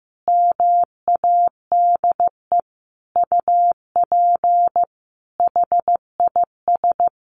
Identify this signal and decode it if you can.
Morse code, heard as a keyed tone, about 15 wpm, 700 Hz. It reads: MADE UP HIS
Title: Sentences from Top 1400 Words 15wpm